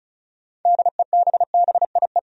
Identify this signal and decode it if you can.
Morse code, heard as a keyed tone, about 36 wpm, 700 Hz. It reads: DEBBIE
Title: USA Common Names - 35wpm